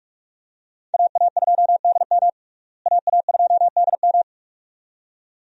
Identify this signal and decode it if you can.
Morse code, heard as a keyed tone, about 42 wpm, 700 Hz. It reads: AA2DM AA2DM
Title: USA Call Signs - 5 Characters - Repeated 2x - 45wpm